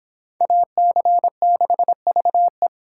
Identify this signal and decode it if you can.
Morse code, heard as a keyed tone, about 25 wpm, 700 Hz. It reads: AC6VE